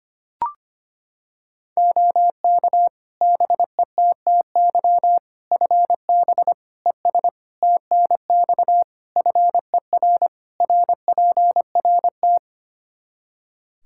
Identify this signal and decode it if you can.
Morse code, heard as a keyed tone, about 25 wpm, 700 Hz. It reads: OK BETTY FB ES TNX FER RPRT
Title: Practice QSO - Exchange 2 - Info - 25wpm